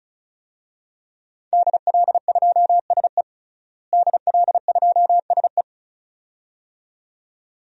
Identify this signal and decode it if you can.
Morse code, heard as a keyed tone, about 36 wpm, 700 Hz. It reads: DL2SE DL2SE
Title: DXCC-100 Call Signs - 5 Characters - Repeated 2x - 35wpm